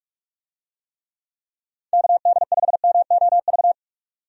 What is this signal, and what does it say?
Morse code, heard as a keyed tone, about 43 wpm, 700 Hz. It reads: KD5MOV
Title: USA Call Signs - 6 Characters - 45wpm